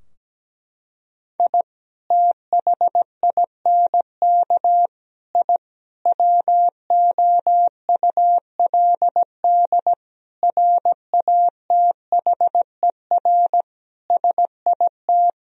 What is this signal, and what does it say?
Morse code, heard as a keyed tone, about 17 wpm, 700 Hz. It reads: I THINK I WOULD RATHER SIT
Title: Sentences from Top 1000 Words - Review Part 4 - 17wpm